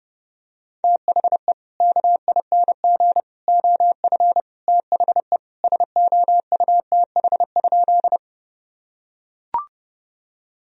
Morse code, heard as a keyed tone, about 30 wpm, 700 Hz. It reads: THE KING OF THE SOUTH?
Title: Sentences from Top 1000 Words - Review Part 2 - 30wpm